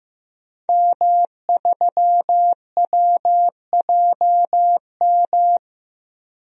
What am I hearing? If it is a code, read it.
Morse code, heard as a keyed tone, about 15 wpm, 700 Hz. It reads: M3WJM
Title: DXCC-100 Call Signs - 5 Characters - 15wpm